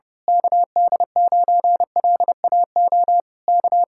Morse code, heard as a keyed tone, about 30 wpm, 700 Hz. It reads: KD9LAO K
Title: Speed-Racing Common QSO Phrases - 25wpm